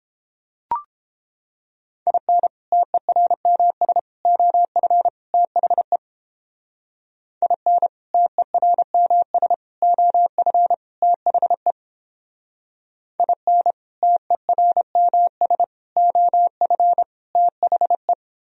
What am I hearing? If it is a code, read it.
Morse code, heard as a keyed tone, about 29 wpm, 700 Hz. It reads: IN TERMS OF THE IN TERMS OF THE IN TERMS OF THE